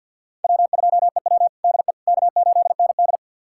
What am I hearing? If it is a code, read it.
Morse code, heard as a keyed tone, about 52 wpm, 700 Hz. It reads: W2EW DE K8ND
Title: USA Callsign DE Callsign - 4 Characters - 50wpm